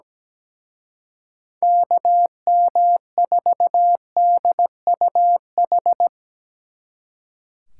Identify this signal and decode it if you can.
Morse code, heard as a keyed tone, about 19 wpm, 700 Hz. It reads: KM4DUH
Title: Speed Racing - USA 2x3 Call Signs 17wpm